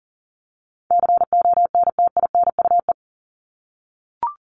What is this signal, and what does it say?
Morse code, heard as a keyed tone, about 40 wpm, 700 Hz. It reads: CONTINUE
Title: Top 751-1000 Words 40wpm